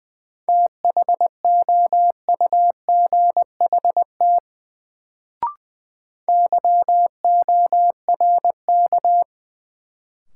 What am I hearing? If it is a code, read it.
Morse code, heard as a keyed tone, about 20 wpm, 700 Hz. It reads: THOUGHT YORK
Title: Sets of 2 Words - New Word Drill - 20wpm